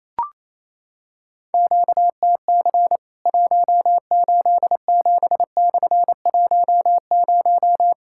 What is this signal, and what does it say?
Morse code, heard as a keyed tone, about 28 wpm, 700 Hz. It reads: QTC 187/10
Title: Worked All Europe DX Contest - QTC Exchange - 28wpm